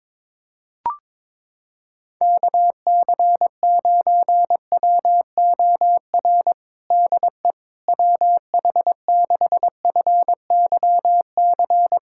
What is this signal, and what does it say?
Morse code, heard as a keyed tone, about 22 wpm, 700 Hz. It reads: KC9WOR DE WH6FYC
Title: USA Callsign DE Callsign 22wpm